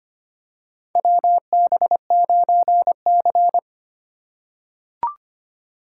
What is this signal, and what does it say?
Morse code, heard as a keyed tone, about 25 wpm, 700 Hz. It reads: WB9C
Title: USA Call Signs - 4 Characters - 3x Character Spacing - 25wpm